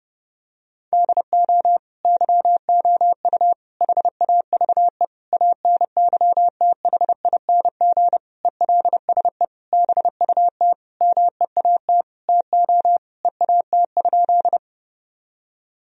Morse code, heard as a keyed tone, about 31 wpm, 700 Hz. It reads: DO YOU HAVE ANYTHING ELSE BUT MEAT TO EAT?
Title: Sentences from Top 1000 Words - Review Part 3 - 30wpm